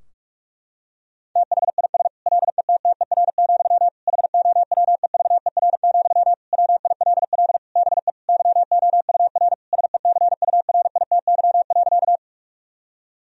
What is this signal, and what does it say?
Morse code, heard as a keyed tone, about 41 wpm, 700 Hz. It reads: THIS LETTER, HOWEVER, WILL BE YOUR SECURITY.
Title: Sentences from Top 1000 Words - Review Part 4 - 45wpm